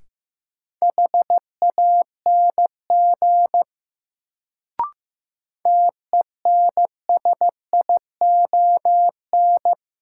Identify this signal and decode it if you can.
Morse code, heard as a keyed tone, about 16 wpm, 700 Hz. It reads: HANG TENSION
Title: Sentences from Top 1600 Words - New Word Drill - 15wpm